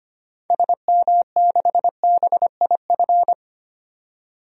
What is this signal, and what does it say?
Morse code, heard as a keyed tone, about 25 wpm, 700 Hz. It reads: SM6BIF
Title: DXCC-100 Call Signs - 6 Characters - 25wpm